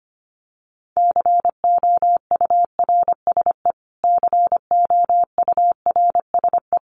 Morse code, heard as a keyed tone, about 25 wpm, 700 Hz. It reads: COURSE COURSE
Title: Top 500 Words Repeated 25wpm